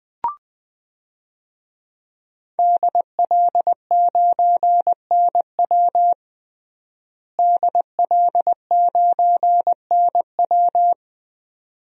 Morse code, heard as a keyed tone, about 20 wpm, 700 Hz. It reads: DL9NW DL9NW
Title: DXCC-100 Call Signs - 5 Characters - Repeated 2x - 20wpm